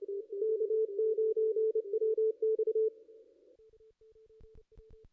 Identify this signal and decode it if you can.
Morse code, heard as a keyed tone, about 30 wpm, 440 Hz, with radio noise. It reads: K9WX
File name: contests / ss-cw-k5tr-op-n5rz -2200